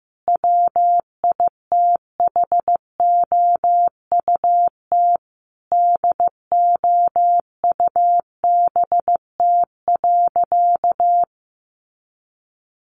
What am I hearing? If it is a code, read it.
Morse code, heard as a keyed tone, about 15 wpm, 700 Hz. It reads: WITHOUT DOUBT.
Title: Sentences from Top 1500 Words 15wpm